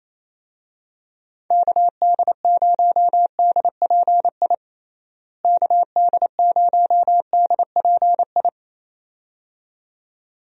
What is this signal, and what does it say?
Morse code, heard as a keyed tone, about 28 wpm, 700 Hz. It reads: KD0DPI KD0DPI
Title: USA Call Signs - 6 Characters - Repeated 2x - 28wpm